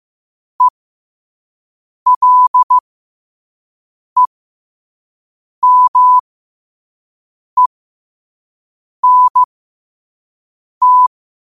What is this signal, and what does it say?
Morse code, heard as a keyed tone, about 15 wpm, spaced as for 5 wpm, 1000 Hz. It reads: ELEMENT
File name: qotdfiles / 5wpm